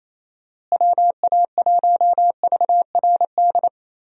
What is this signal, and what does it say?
Morse code, heard as a keyed tone, about 27 wpm, 700 Hz. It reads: WA1VRD
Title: USA Call Signs - 6 Characters - Repeated 3x - 28wpm